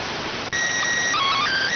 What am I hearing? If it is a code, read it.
SSTV (slow-scan television), PD 290.